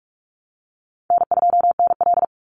Morse code, heard as a keyed tone, about 46 wpm, 700 Hz. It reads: N2NL
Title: Easy Call Signs - 45wpm